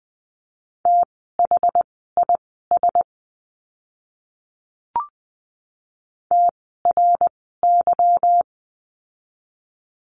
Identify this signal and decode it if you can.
Morse code, heard as a keyed tone, about 20 wpm, spaced as for 14 wpm, 700 Hz. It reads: THIS TRY
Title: 160 Lesson 26f - Letter g - Words 20wpm